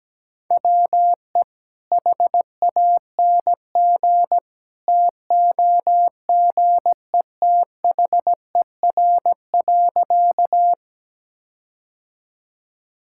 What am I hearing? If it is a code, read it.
Morse code, heard as a keyed tone, about 17 wpm, 700 Hz. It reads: WE HANG TOGETHER.
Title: Sentences from Top 1600 Words 17wpm